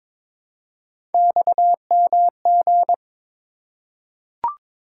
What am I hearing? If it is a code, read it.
Morse code, heard as a keyed tone, about 22 wpm, 700 Hz. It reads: XMG
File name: practice / 22wpm / Three Letter-Number Combinations - 22wpm